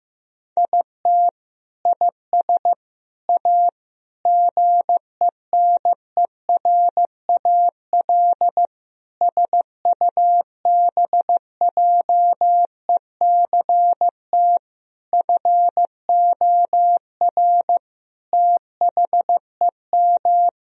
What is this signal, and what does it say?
Morse code, heard as a keyed tone, about 16 wpm, 700 Hz. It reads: IT IS A GENERAL SUBJECT FOR THEM
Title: Sentences from Top 1000 Words - Review Part 3 - 15wpm